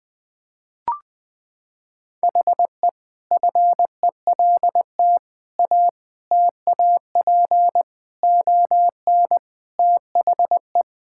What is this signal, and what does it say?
Morse code, heard as a keyed tone, about 20 wpm, 700 Hz. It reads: HE FELT A TAP ON THE
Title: Sentences with 4-Word Phrases - 20wpm